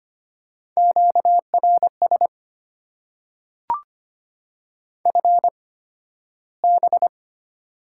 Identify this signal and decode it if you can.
Morse code, heard as a keyed tone, about 25 wpm, spaced as for 13 wpm, 690 Hz. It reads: QRS FB
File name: Top 90 QSO Elements - 8x Character Spacing - 25wpm